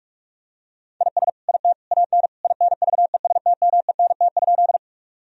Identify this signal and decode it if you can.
Morse code, heard as a keyed tone, about 41 wpm, 700 Hz. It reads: IS IT AN INVESTMENT?
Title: Sentences from Top 1000 Words - Review Part 4 - 45wpm